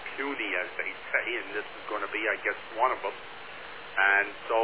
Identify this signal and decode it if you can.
Emergency Alert System (SAME) tones, heard as harsh buzzing bursts, with speech. It is heard from 0.6 to 4.3 s.